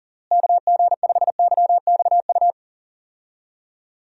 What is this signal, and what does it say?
Morse code, heard as a keyed tone, about 40 wpm, 700 Hz. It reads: KG5YXU
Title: USA Call Signs - 6 Characters - Repeated 3x - 40wpm